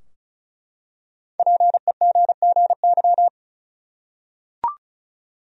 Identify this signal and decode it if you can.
Morse code, heard as a keyed tone, about 36 wpm, 700 Hz. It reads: PEGGY
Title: USA Common Names - 35wpm